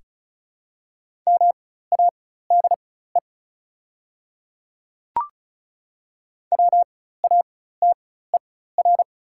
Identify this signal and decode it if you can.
Morse code, heard as a keyed tone, about 34 wpm, spaced as for 16 wpm, 690 Hz. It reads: MADE WATER
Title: 4x Character Spacing - Top 200 Words - 35wpm